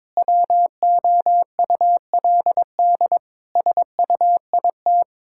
Morse code, heard as a keyed tone, about 21 wpm, 700 Hz. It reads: WOULD SUIT
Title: Sentences from Top 1600 Words 22wpm